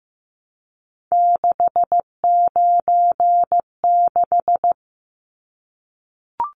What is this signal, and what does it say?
Morse code, heard as a keyed tone, about 15 wpm, 700 Hz. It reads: 696
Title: Numbers - 3-Digit 15wpm